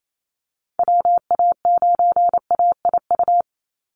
Morse code, heard as a keyed tone, about 28 wpm, 700 Hz. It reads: WA9AIU
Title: USA Call Signs 28wpm